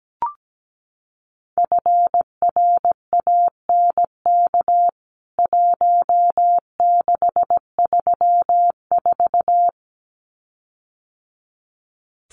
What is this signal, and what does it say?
Morse code, heard as a keyed tone, about 17 wpm, 700 Hz. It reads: FRANK 1634
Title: CWT Exchange 17wpm